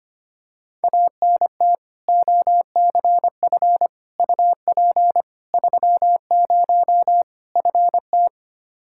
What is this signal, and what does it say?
Morse code, heard as a keyed tone, about 25 wpm, 700 Hz. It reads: ANT OCF UP 30 FT
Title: Practice QSO - Exchange 2 - Info - 25wpm